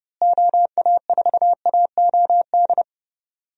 Morse code, heard as a keyed tone, about 32 wpm, 700 Hz. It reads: OA4AOD
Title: USA Call Signs - 6 Characters - Repeated 2x - 30wpm